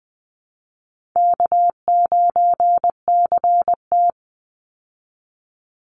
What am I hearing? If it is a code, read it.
Morse code, heard as a keyed tone, about 20 wpm, 700 Hz. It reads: K9CT
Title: Call Signs - 20wpm